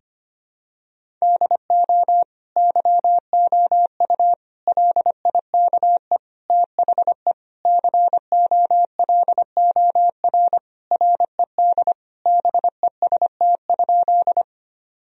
Morse code, heard as a keyed tone, about 25 wpm, 700 Hz. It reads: DO YOU LIKE THE COLOR RED BEST?
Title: Sentences from Top 1000 Words - Review Part 2 - 25wpm